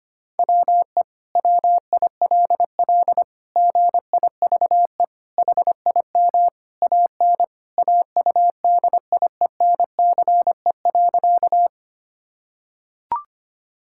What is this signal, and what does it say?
Morse code, heard as a keyed tone, about 25 wpm, 700 Hz. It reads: WE WILL GIVE HIM AN AUDIENCE.
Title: Sentences from Top 1000 Words - Review Part 5 - 25wpm